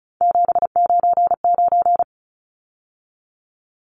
Morse code, heard as a keyed tone, about 34 wpm, 700 Hz. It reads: 799
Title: Numbers - 3-Digit 35wpm